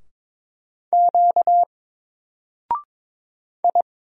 Morse code, heard as a keyed tone, about 22 wpm, 700 Hz. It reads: Q I
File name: Single-Letter-Number-22wpm